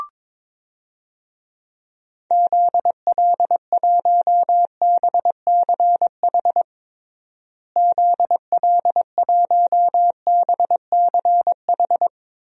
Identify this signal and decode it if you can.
Morse code, heard as a keyed tone, about 22 wpm, 700 Hz. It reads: ZL1BCH ZL1BCH
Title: DXCC-100 Call Signs - 6 Characters - Repeated 2x - 22wpm